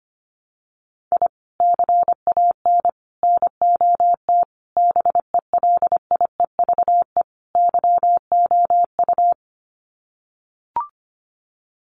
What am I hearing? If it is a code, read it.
Morse code, heard as a keyed tone, about 25 wpm, 700 Hz. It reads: I CAN NOT BELIEVE YOU
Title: Sets of 5 Words Encore 25wpm